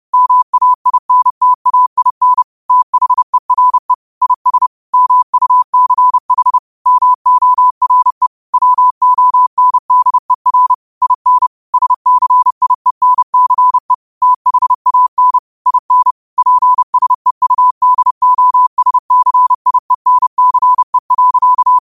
Morse code, heard as a keyed tone, about 30 wpm, 1000 Hz. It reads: MAINTAIN THERE IS MUCH MORE WONDER IN SCIENCE THAN IN PSEUDOSCIENCE.
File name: qotdfiles / 30wpm